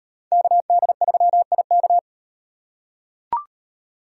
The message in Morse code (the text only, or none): KD3IK